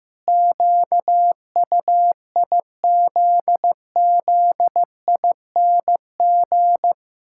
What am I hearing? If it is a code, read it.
Morse code, heard as a keyed tone, about 15 wpm, 700 Hz. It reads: QUIZZING
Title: Pangram Challenge v2 - 15wpm